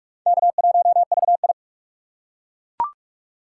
Morse code, heard as a keyed tone, about 42 wpm, 700 Hz. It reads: K1VI
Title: USA Call Signs - 3x Character Spacing - 45wpm